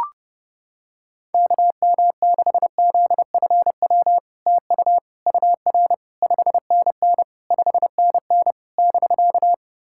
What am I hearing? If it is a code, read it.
Morse code, heard as a keyed tone, about 29 wpm, 700 Hz. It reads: KM6ZFW TU UR 5NN 5NN <BK>
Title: SOTA QSO Practice 30wpm